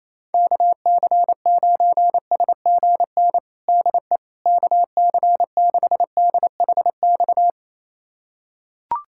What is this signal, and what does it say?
Morse code, heard as a keyed tone, about 28 wpm, 700 Hz. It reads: KC9SGN DE KC6DHX
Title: USA Callsign DE Callsign 28wpm